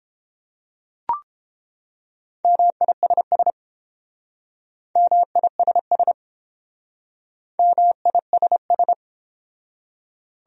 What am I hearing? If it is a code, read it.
Morse code, heard as a keyed tone, about 30 wpm, 700 Hz. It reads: MISS MISS MISS